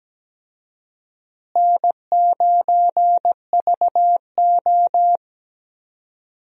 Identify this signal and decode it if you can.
Morse code, heard as a keyed tone, about 17 wpm, 700 Hz. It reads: N9VO
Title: USA Call Signs - 4 Characters - Repeated 2x - 17wpm